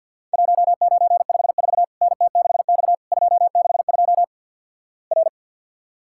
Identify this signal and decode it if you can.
Morse code, heard as a keyed tone, about 50 wpm, 700 Hz. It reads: 1954 NT6X 262 R
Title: Worked All Europe DX Contest - QTC Exchange - 50wpm